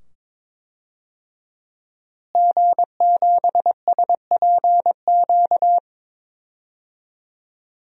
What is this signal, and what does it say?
Morse code, heard as a keyed tone, about 22 wpm, 700 Hz. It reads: G7SPQ
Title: DXCC-100 Call Signs - 5 Characters - Repeated 2x - 22wpm